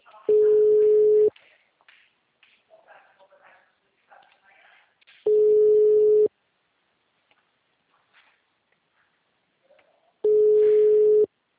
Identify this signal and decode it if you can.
Ringback tone (Europe, 425 Hz), with speech.